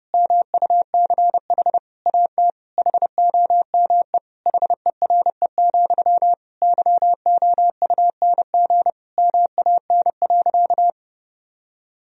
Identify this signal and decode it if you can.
Morse code, heard as a keyed tone, about 30 wpm, 700 Hz. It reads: MUCH AT HOME HERE, YOUNG MAN.
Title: Sentences from Top 1000 Words - Review Part 1 - 30wpm